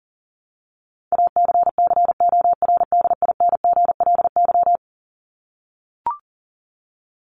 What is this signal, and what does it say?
Morse code, heard as a keyed tone, about 40 wpm, 700 Hz. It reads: ACCORDINGLY
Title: LY Suffix - Top 100 Words with Base Words 40wpm